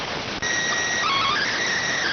SSTV (slow-scan television), PD 290.